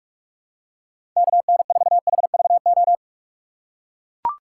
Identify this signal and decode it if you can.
Morse code, heard as a keyed tone, about 42 wpm, 700 Hz. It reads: KN4HVO
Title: USA Call Signs - 4x Character Spacing - 45wpm